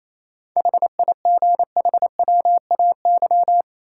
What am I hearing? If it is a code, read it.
Morse code, heard as a keyed tone, about 28 wpm, 700 Hz. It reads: HIGHWAY
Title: Sentences from Top 1400 Words - New Word Drill - 28wpm